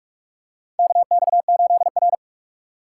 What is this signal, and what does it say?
Morse code, heard as a keyed tone, about 46 wpm, 700 Hz. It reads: KX8R
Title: USA Call Signs - 4 Characters - 4x Character Spacing - 45wpm